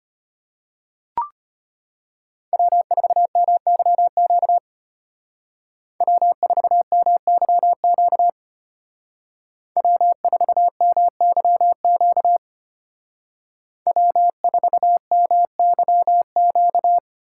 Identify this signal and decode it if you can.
Morse code, heard as a keyed tone, about 33 wpm, 700 Hz. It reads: W4MYQ W4MYQ W4MYQ W4MYQ